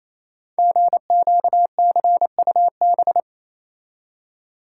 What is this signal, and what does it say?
Morse code, heard as a keyed tone, about 28 wpm, 700 Hz. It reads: GQCUB
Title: Code Groups - 5 Letter:Numbers - 28wpm